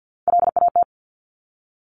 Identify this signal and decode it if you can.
Morse code, heard as a keyed tone, about 50 wpm, 700 Hz. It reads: LAT
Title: Top Three Letter Combinations in Words 50wpm